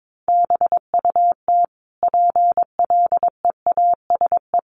Morse code, heard as a keyed tone, about 22 wpm, 700 Hz. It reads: BUT PLEASE
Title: Sentences from Top 1500 Words 22wpm